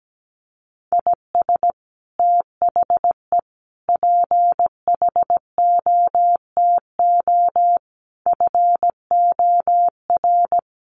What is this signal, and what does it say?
Morse code, heard as a keyed tone, about 17 wpm, 700 Hz. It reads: IS THE PHOTO FOR
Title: Sentences from Top 1200 Words 17wpm